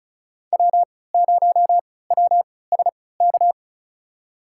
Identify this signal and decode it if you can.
Morse code, heard as a keyed tone, about 34 wpm, spaced as for 20 wpm, 700 Hz. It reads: W0WSK
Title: USA Call Signs - 3x Character Spacing - 35wpm